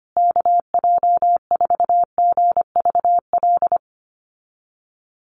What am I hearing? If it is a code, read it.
Morse code, heard as a keyed tone, about 25 wpm, 700 Hz. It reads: KJ4GVL
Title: USA Call Signs Repeated 3x 25wpm